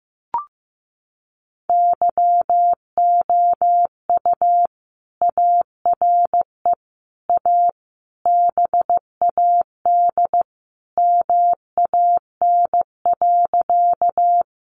Morse code, heard as a keyed tone, about 15 wpm, 700 Hz. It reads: YOU ARE A BAD MAN.
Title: Sentences from Top 700 Words 15wpm